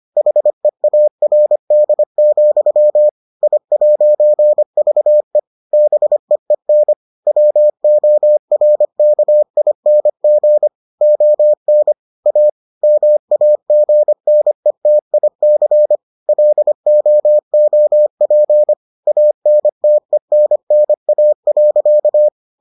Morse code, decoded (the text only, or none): HEARD, I'VE BEEN WORKING ON A MAGNETIC LOOP ANTENNA.